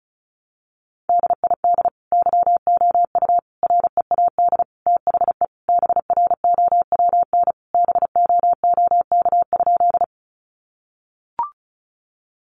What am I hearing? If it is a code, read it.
Morse code, heard as a keyed tone, about 34 wpm, 700 Hz. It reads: DID YOU READ THE BROWN BOOK?
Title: Sentences from Top 1200 Words 35wpm